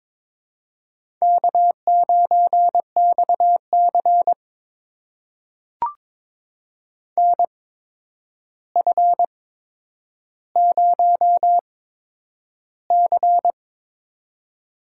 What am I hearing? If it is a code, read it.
Morse code, heard as a keyed tone, about 22 wpm, spaced as for 9 wpm, 700 Hz. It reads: K9XC NF0C
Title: USA Call Signs - 4 Characters - 8x Character Spacing - 22wpm